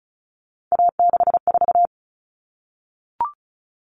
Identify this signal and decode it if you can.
Morse code, heard as a keyed tone, about 34 wpm, 700 Hz. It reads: A64
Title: Common Cut Numbers - 3-Digit 35wpm